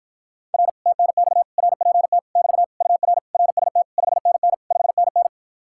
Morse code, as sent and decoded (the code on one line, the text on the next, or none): .-. / - -. -..- / .-. .--. - / -...- / ..- .-. / .-. ... - / ..... -. -. / ..... -. -.
R TNX RPT <BT> UR RST 5NN 5NN